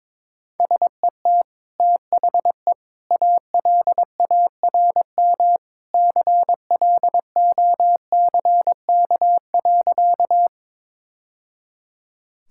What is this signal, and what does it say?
Morse code, heard as a keyed tone, about 22 wpm, 700 Hz. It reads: SET THE ALARM CLOCK.
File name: practice / 22wpm / Easy Sentences with Hints - 22wpm